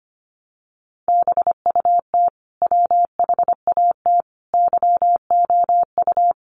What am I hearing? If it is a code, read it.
Morse code, heard as a keyed tone, about 25 wpm, 700 Hz. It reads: BUT WHAT YOU
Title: Sentences from Top 1300 Words 25wpm